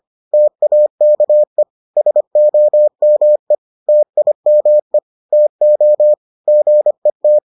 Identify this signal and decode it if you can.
Morse code, heard as a keyed tone, about 25 wpm, 600 Hz. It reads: TAKE SOME TIME TO GET